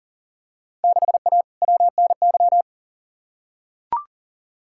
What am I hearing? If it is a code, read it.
Morse code, heard as a keyed tone, about 40 wpm, 700 Hz. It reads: 6A WNY
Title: Field Day Exchanges Encore - 40wpm